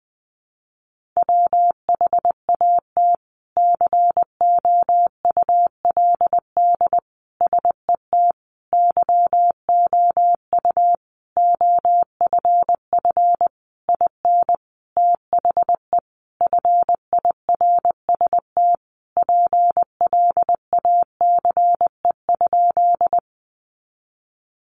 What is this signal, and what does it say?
Morse code, heard as a keyed tone, about 20 wpm, 700 Hz. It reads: WHAT COULD SET YOU OFF IN THE FIRST PLACE?
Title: Sentences from Top 200 Words 20wpm